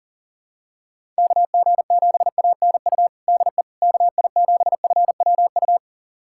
Morse code, heard as a keyed tone, about 40 wpm, 700 Hz. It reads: KG7ANU DE KI7FWU
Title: USA Callsign DE Callsign - 6 Characters - 40wpm